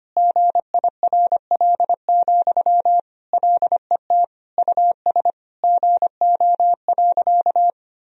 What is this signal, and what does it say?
Morse code, heard as a keyed tone, about 25 wpm, 700 Hz. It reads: GIRL, LET US GO.
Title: Sentences from Top 400 Words and Abbreviations 25wpm